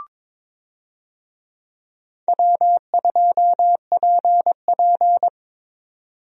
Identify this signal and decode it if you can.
Morse code, heard as a keyed tone, about 21 wpm, 700 Hz. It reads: W2PP
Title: USA Call Signs - 4 Characters - Repeated 3x - 22wpm